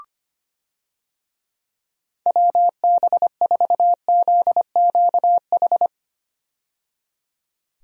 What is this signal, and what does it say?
Morse code, heard as a keyed tone, about 25 wpm, 700 Hz. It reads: WB4ZQH
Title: USA Call Signs - 6 Characters - 25wpm